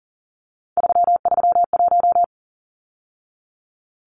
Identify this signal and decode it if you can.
Morse code, heard as a keyed tone, about 40 wpm, 700 Hz. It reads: 331
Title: Numbers - 3-Digit 40wpm